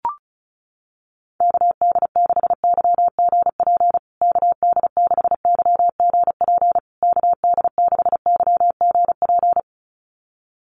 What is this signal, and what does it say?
Morse code, heard as a keyed tone, about 34 wpm, 700 Hz. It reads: KD6YGP KD6YGP KD6YGP
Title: USA Call Signs Repeated 3x 35wpm